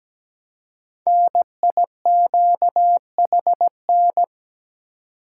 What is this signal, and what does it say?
Morse code, heard as a keyed tone, about 17 wpm, 700 Hz. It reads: NIQHN
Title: Code Groups - 5 Letter:Numbers - 17wpm